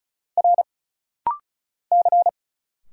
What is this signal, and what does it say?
Morse code, heard as a keyed tone, about 36 wpm, 700 Hz. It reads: R C
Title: Single Letter-Number Alt - Rapid-Fire 35wpm